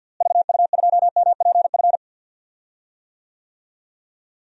Morse code, heard as a keyed tone, about 45 wpm, 700 Hz. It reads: VU2MPF
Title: DXCC-100 Call Signs - 6 Characters - Repeated 2x - 50wpm